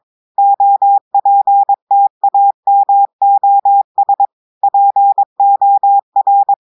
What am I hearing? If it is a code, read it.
Morse code, heard as a keyed tone, about 22 wpm, 800 Hz. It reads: OPTAMOS POR
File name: audios / portugais / portugais texte en clair 1 vitesse 22 mots minutes